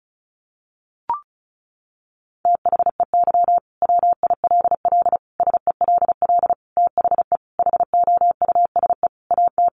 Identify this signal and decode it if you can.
Morse code, heard as a keyed tone, about 34 wpm, 700 Hz. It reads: THEY WILL SELL THE HOUSE AT
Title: Sentences from Top 1200 Words 35wpm